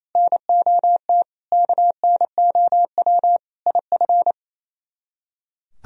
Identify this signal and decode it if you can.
Morse code, heard as a keyed tone, about 35 wpm, 700 Hz. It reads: NOT KNOW IF
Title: Speed-Racing - Sets of 5 Words 28wpm